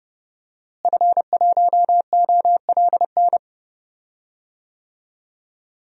Morse code, heard as a keyed tone, about 30 wpm, 700 Hz. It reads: F1OLN
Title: DXCC-100 Call Signs - 5 Characters - Repeated 3x - 30wpm